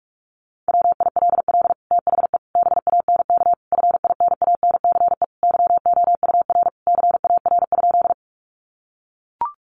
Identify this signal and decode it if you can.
Morse code, heard as a keyed tone, about 44 wpm, 700 Hz. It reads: WILL THE BANK FINANCE YOUR CAR?
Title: Sentences from Top 1500 Words 45wpm